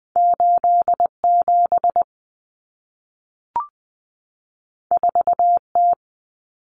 Morse code, heard as a keyed tone, about 20 wpm, 700 Hz. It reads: 87 4T
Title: Common Cut Numbers - 2-Digit 20wpm